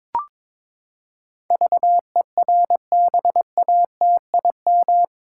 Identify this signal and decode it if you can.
Morse code, heard as a keyed tone, about 22 wpm, 700 Hz. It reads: VERBATIM
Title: Pangram Challenge v2 - 22wpm